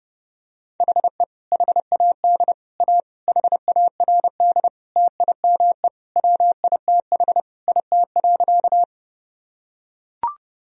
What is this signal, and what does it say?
Morse code, heard as a keyed tone, about 30 wpm, 700 Hz. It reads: HE HAD A HARD TIME WITH IT.
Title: Sentences from Top 1000 Words - Review Part 1 - 30wpm